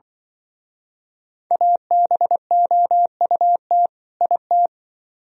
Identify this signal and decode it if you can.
Morse code, heard as a keyed tone, about 24 wpm, 700 Hz. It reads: ABOUT IT
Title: Speed-Racing - Sets of 2 Words 20wpm